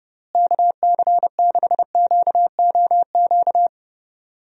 Morse code, heard as a keyed tone, about 30 wpm, 700 Hz. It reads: KC6QOQ
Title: USA Call Signs - 6 Characters - Repeated 3x - 30wpm